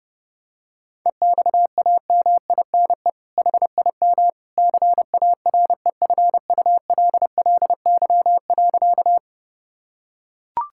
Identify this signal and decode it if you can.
Morse code, heard as a keyed tone, about 30 wpm, 700 Hz. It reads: EXAMINE HIM CAREFULLY.
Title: Sentences from Top 1600 Words 30wpm